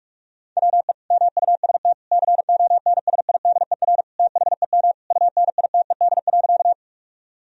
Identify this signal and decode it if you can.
Morse code, heard as a keyed tone, about 46 wpm, 700 Hz. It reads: WE MUST CONSIDER THEM UNITED.
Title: Sentences from Top 1000 Words - Review Part 4 - 45wpm